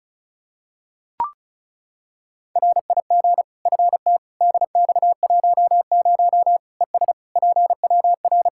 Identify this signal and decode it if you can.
Morse code, heard as a keyed tone, about 34 wpm, 700 Hz. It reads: RIG FT DX10 ES PWR
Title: Practice QSO - Exchange 2 - Info - 35wpm